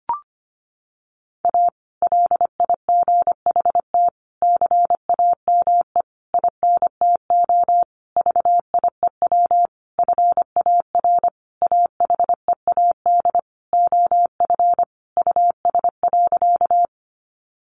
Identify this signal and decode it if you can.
Morse code, heard as a keyed tone, about 25 wpm, 700 Hz. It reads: A LIGHT CAME INTO VIEW FAR AHEAD OF US.
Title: Sentences from Top 1100 Words 25wpm